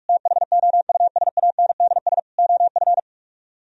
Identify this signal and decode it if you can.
Morse code, heard as a keyed tone, about 59 wpm, spaced as for 51 wpm, 700 Hz. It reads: THOUSANDS OF
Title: Speed-Racing - Sets of 4 Words 45wpm